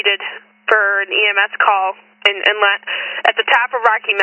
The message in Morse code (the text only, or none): none